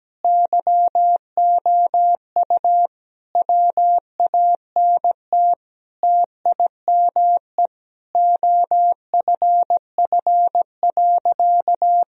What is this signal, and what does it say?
Morse code, heard as a keyed tone, about 17 wpm, 700 Hz. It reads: YOU WANT TIME OFF.
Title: Sentences from Top 1000 Words - Review Part 3 - 17wpm